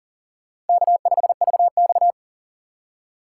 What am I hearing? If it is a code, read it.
Morse code, heard as a keyed tone, about 48 wpm, 700 Hz. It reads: K5VX
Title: Speed Racing - USA 1x2 Call Signs 40wpm